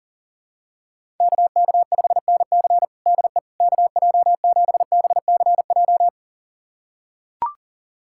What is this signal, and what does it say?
Morse code, heard as a keyed tone, about 40 wpm, 700 Hz. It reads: KK5NC DE KJ7BCJ
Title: USA Callsign DE Callsign 40wpm